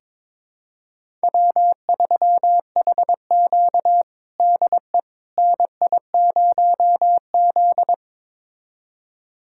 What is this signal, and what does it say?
Morse code, heard as a keyed tone, about 22 wpm, 700 Hz. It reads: W3HQ DE NI0Z
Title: USA Callsign DE Callsign - 4 Characters - 22wpm